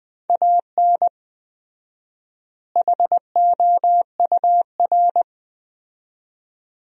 Morse code, heard as a keyed tone, about 20 wpm, 700 Hz. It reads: AN HOUR
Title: 4x Word-Spacing - Sentences from Top 400 Words - 20wpm